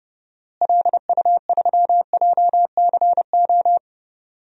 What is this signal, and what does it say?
Morse code, heard as a keyed tone, about 30 wpm, 700 Hz. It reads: LU3JCO
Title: DXCC-100 Call Signs - 6 Characters - Repeated 3x - 30wpm